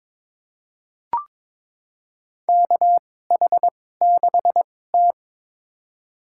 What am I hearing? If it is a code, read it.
Morse code, heard as a keyed tone, about 22 wpm, spaced as for 16 wpm, 700 Hz. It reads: KH6T
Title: USA Call Signs - 4 Characters - 2x Character Spacing - 22wpm